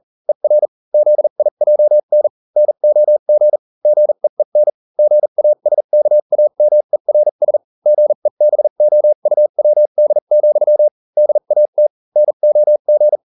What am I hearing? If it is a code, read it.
Morse code, heard as a keyed tone, about 40 wpm, spaced as for 35 wpm, 600 Hz. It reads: ER ZIJN NOG GEEN GASKAMERS GEBOUWD, DAT NOG